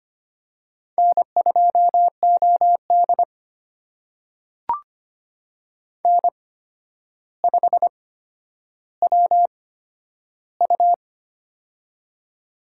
Morse code, heard as a keyed tone, about 25 wpm, spaced as for 11 wpm, 700 Hz. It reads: N2OD N5WU
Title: USA Call Signs - 4 Characters - 8x Character Spacing - 25wpm